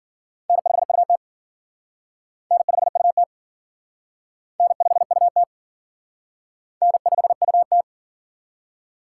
Morse code, heard as a keyed tone, about 50 wpm, 700 Hz. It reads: N5UT N5UT N5UT N5UT
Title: Speed Racing - USA 1x2 Call Signs 40wpm